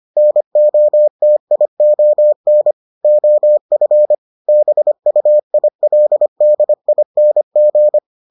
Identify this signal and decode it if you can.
Morse code, heard as a keyed tone, about 25 wpm, 600 Hz. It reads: NOTION OF BUILDING